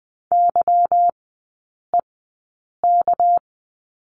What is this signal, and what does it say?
Morse code, heard as a keyed tone, about 20 wpm, spaced as for 8 wpm, 690 Hz. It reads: YEK
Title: 212 Lesson 34e - Number 8 - Three Characters 20wpm